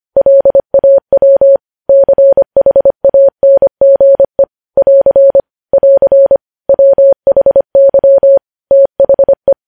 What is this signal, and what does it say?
Morse code, heard as a keyed tone, about 25 wpm, 550 Hz. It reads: LAW CHANGE <AR> <AR> WHY THE
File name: podcast / News headlines-1x-25x25